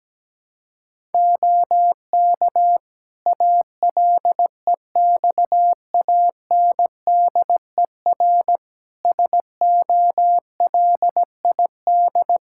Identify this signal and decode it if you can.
Morse code, heard as a keyed tone, about 17 wpm, 700 Hz. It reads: OK ALEXANDER SOLID
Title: K1USN SST QSO Practice 17wpm